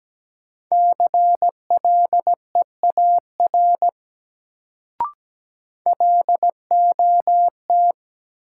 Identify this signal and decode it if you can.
Morse code, heard as a keyed tone, about 17 wpm, 700 Hz. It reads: CLEAR LOT
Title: Sentences from Top 500 Words - New Word Drill - 17wpm